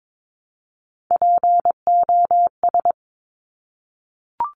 Morse code, heard as a keyed tone, about 22 wpm, 700 Hz. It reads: POS
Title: Top Three Letter Combinations in Words 22wpm